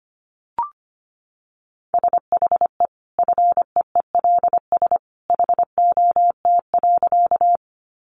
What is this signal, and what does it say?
Morse code, heard as a keyed tone, about 25 wpm, 700 Hz. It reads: SHE FEELS HOT.
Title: Additional Easy Phrases 25wpm